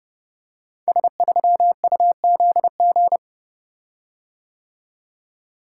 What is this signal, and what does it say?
Morse code, heard as a keyed tone, about 29 wpm, 700 Hz. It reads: S3UZG
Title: Code Groups - 5 Letter:Numbers - 30wpm